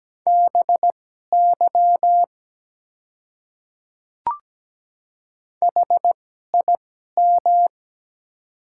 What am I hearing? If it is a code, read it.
Morse code, heard as a keyed tone, about 17 wpm, spaced as for 12 wpm, 700 Hz. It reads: BY HIM
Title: 2x Character Spacing - Top 100 Words - 17wpm